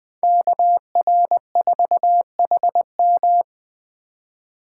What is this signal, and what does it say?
Morse code, heard as a keyed tone, about 23 wpm, 700 Hz. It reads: KR4HM
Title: Speed Racing - USA 2x2 Call Signs 20wpm